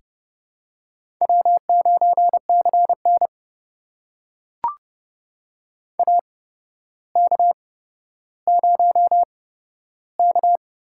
Morse code, heard as a keyed tone, about 29 wpm, spaced as for 13 wpm, 700 Hz. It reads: W9CN AK0K
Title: USA Call Signs - 4 Characters - 8x Character Spacing - 30wpm